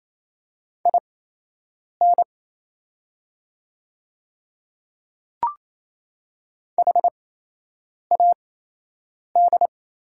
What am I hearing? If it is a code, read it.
Morse code, heard as a keyed tone, about 28 wpm, spaced as for 8 wpm, 700 Hz. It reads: IN HAD
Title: 8x Character Spacing - Top 100 Words - 28wpm